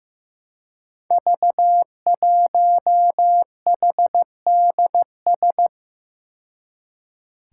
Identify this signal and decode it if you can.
Morse code, heard as a keyed tone, about 15 wpm, 700 Hz. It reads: V1HDS
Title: Five Challenging Letter-Number Combinations 15wpm